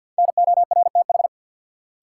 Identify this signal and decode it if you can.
Morse code, heard as a keyed tone, about 47 wpm, 700 Hz. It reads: NORTH
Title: Sets of 5 Words Encore - New Word Drill - 50wpm